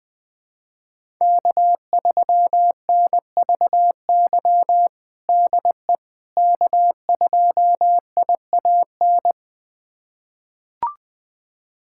Morse code, heard as a keyed tone, about 21 wpm, 700 Hz. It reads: K3NVY DE K2IAN
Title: USA Callsign DE Callsign - 5 Characters - 20wpm